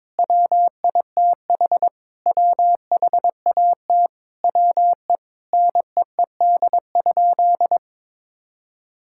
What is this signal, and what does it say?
Morse code, heard as a keyed tone, about 22 wpm, 700 Hz. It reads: WITH WHAT WE NEED?
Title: Sentences from Top 1000 Words - Review Part 5 - 22wpm